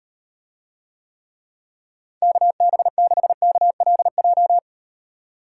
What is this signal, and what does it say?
Morse code, heard as a keyed tone, about 38 wpm, 700 Hz. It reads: KB6KLJ